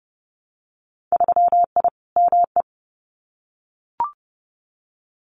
Morse code, heard as a keyed tone, about 30 wpm, 700 Hz. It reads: 3I ME
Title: Winter Field Day Exchanges 30wpm